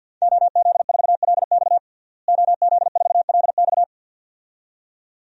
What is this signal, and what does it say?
Morse code, heard as a keyed tone, about 51 wpm, 700 Hz. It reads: OZ4LX OZ4LX
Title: DXCC-100 Call Signs - 5 Characters - Repeated 2x - 50wpm